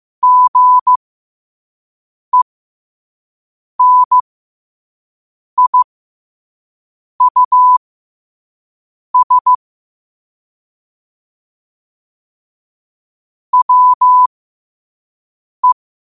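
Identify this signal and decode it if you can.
Morse code, heard as a keyed tone, about 15 wpm, spaced as for 5 wpm, 1000 Hz. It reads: GENIUS WE
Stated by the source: Morse code at 5 wpm.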